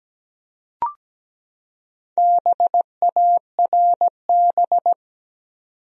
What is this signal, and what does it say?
Morse code, heard as a keyed tone, about 17 wpm, 700 Hz. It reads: BARB
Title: USA Common Names - Max 5 Letters - 17wpm